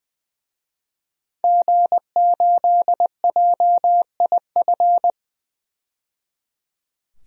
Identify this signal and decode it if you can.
Morse code, heard as a keyed tone, about 20 wpm, 700 Hz. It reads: G8JIF
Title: DXCC-100 Call Signs - 5 Characters - 20wpm